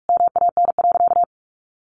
Morse code, heard as a keyed tone, about 46 wpm, 700 Hz. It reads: MAN.
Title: Sentences from Top 400 Words 45wpm